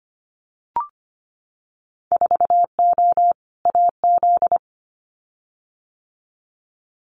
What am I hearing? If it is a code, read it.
Morse code, heard as a keyed tone, about 25 wpm, 700 Hz. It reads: 4O AZ
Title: Winter Field Day Exchanges Encore 25wpm